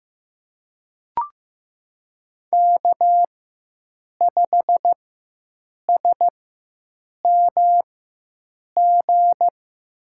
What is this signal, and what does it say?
Morse code, heard as a keyed tone, about 15 wpm, spaced as for 7 wpm, 700 Hz. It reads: K5SMG
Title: USA Call Signs - 5 Characters - 4x Character Spacing - 15wpm